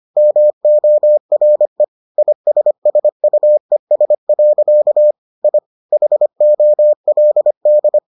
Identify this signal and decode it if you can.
Morse code, heard as a keyed tone, about 25 wpm, 600 Hz. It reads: MORE ISSUES. I HOLD